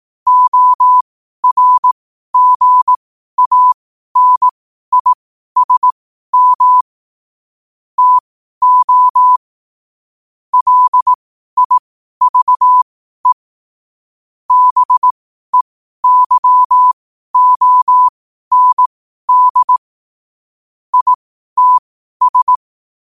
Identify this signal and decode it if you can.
Morse code, heard as a keyed tone, about 18 wpm, spaced as for 12 wpm, 1000 Hz. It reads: ORGANISM TO LIVE BEYOND ITS
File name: qotdfiles / 12wpm